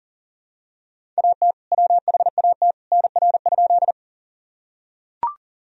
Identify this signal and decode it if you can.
Morse code, heard as a keyed tone, about 40 wpm, 700 Hz. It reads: AT WHAT NR?
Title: Sentences from Top 100 Words and Abbreviations 40wpm